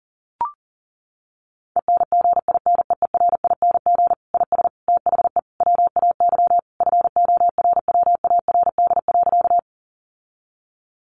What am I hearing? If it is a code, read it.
Morse code, heard as a keyed tone, about 40 wpm, 700 Hz. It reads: ENGINEERING IS THE WAY FORWARD.
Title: Sentences from Top 1500 Words 40wpm